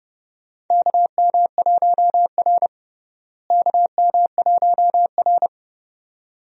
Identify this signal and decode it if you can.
Morse code, heard as a keyed tone, about 31 wpm, 700 Hz. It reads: KM1R KM1R
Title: USA Call Signs - 4 Characters - Repeated 2x - 30wpm